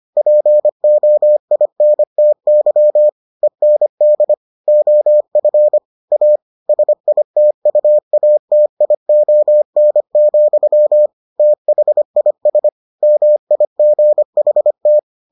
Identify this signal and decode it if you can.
Morse code, heard as a keyed tone, about 25 wpm, 600 Hz. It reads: POINTY END OF A SITUATION, THIS MIGHT